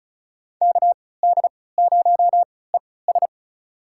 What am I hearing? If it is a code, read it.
Morse code, heard as a keyed tone, about 36 wpm, spaced as for 20 wpm, 700 Hz. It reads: KD0ES
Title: USA Call Signs - 5 Characters - 3x Character Spacing - 35wpm